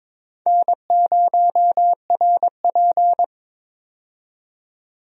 Morse code, heard as a keyed tone, about 22 wpm, 700 Hz. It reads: N0RP
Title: USA Call Signs - 4 Characters - 22wpm